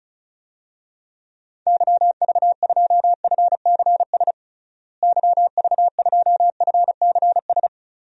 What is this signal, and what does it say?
Morse code, heard as a keyed tone, about 36 wpm, 700 Hz. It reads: YV2FCS YV2FCS
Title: DXCC-100 Call Signs - 6 Characters - Repeated 2x - 35wpm